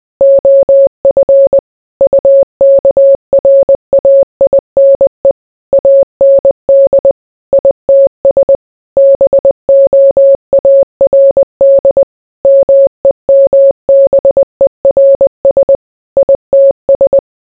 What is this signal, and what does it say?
Morse code, heard as a keyed tone, about 20 wpm, 550 Hz. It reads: OF UKRAINE AND ITS BOARD MEMBERS ITS